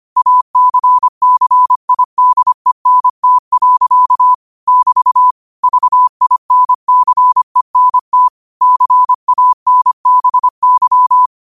Morse code, heard as a keyed tone, about 25 wpm, 1000 Hz. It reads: ACCIDENT. <BT> VINCENT CANBY
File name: 25wpm